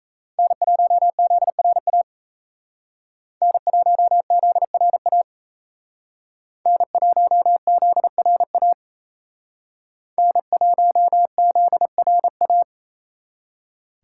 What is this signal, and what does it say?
Morse code, heard as a keyed tone, about 35 wpm, 700 Hz. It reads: N1ZRA N1ZRA N1ZRA N1ZRA